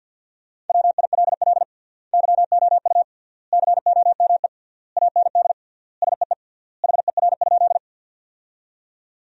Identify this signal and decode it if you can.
Morse code, heard as a keyed tone, about 50 wpm, 700 Hz. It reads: WILL YOU COME AND SEE HER?
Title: 3x Word-Spacing - Sentences from Top 100 Words - 50wpm